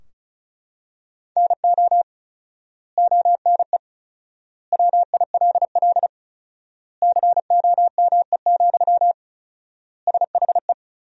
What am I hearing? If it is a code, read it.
Morse code, heard as a keyed tone, about 34 wpm, 700 Hz. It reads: NO ONE WILL COME, SHE
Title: 4x Word-Spacing - Sentences from Top 100 Words - 35wpm